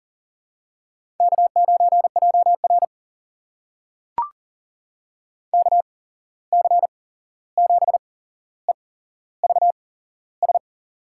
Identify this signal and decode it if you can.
Morse code, heard as a keyed tone, about 40 wpm, spaced as for 15 wpm, 700 Hz. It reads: K9JR KC7EVS
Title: USA Call Signs - 8x Character Spacing - 40wpm